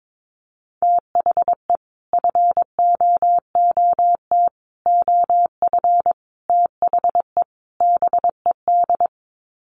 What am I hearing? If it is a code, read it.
Morse code, heard as a keyed tone, about 22 wpm, 700 Hz. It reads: THE FOOT OF THE BED
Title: Sets of 5 Words Encore2 22wpm